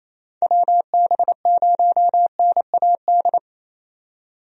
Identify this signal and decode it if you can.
Morse code, heard as a keyed tone, about 27 wpm, 700 Hz. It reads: WB0NAD
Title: USA Call Signs - 6 Characters - 28wpm